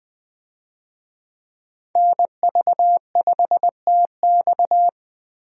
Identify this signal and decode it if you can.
Morse code, heard as a keyed tone, about 20 wpm, 700 Hz. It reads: NV5TX
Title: USA Call Signs - 5 Characters - Repeated 2x - 20wpm